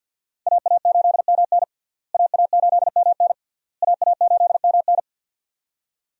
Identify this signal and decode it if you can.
Morse code, heard as a keyed tone, about 47 wpm, 700 Hz. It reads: AA8MN AA8MN AA8MN
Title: USA Call Signs - 5 Characters - Repeated 3x - 50wpm